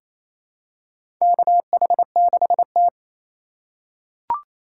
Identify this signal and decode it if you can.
Morse code, heard as a keyed tone, about 28 wpm, 700 Hz. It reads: KH6T
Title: USA Call Signs - 4 Characters - 3x Character Spacing - 28wpm